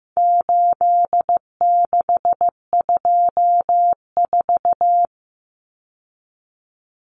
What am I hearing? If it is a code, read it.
Morse code, heard as a keyed tone, about 15 wpm, 700 Hz. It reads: 8624
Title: Common Cut Numbers - 4-Digit 15wpm